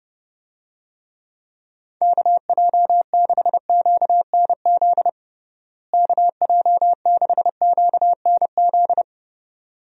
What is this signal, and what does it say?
Morse code, heard as a keyed tone, about 30 wpm, 700 Hz. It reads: KJ6QNZ KJ6QNZ
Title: USA Call Signs - 6 Characters - Repeated 2x - 30wpm